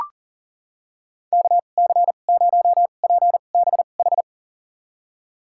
Morse code, heard as a keyed tone, about 40 wpm, spaced as for 29 wpm, 700 Hz. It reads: KC0PBH
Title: USA Call Signs - 6 Characters - 2x Character Spacing - 40wpm